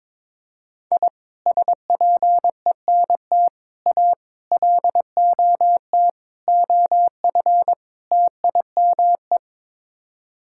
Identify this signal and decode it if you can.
Morse code, heard as a keyed tone, about 22 wpm, 700 Hz. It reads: I SPENT A LOT OF TIME
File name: practice / 22wpm / Sets of 6 Words - 22wpm